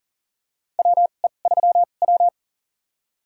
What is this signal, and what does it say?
Morse code, heard as a keyed tone, about 38 wpm, spaced as for 28 wpm, 700 Hz. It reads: WE3W
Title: USA Call Signs - 4 Characters - 2x Character Spacing - 40wpm